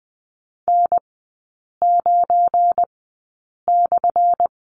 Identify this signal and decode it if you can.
Morse code, heard as a keyed tone, about 20 wpm, spaced as for 8 wpm, 700 Hz. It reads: N9/
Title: 178 Lesson 29e - Slash - Three Characters 20wpm